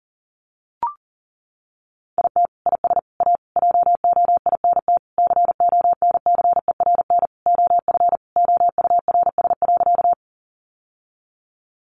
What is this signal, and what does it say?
Morse code, heard as a keyed tone, about 40 wpm, 700 Hz. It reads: IT IS A JOINT CONCERN OF OURS.
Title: Sentences from Top 1300 Words 40wpm